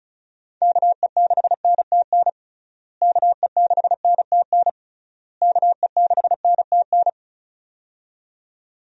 Morse code, heard as a keyed tone, about 36 wpm, 700 Hz. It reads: KE6NTN KE6NTN KE6NTN
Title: USA Call Signs - 6 Characters - Repeated 3x - 35wpm